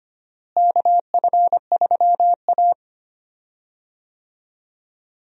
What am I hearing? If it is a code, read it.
Morse code, heard as a keyed tone, about 25 wpm, 700 Hz. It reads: KF3A
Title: USA Call Signs - 4 Characters - Repeated 2x - 25wpm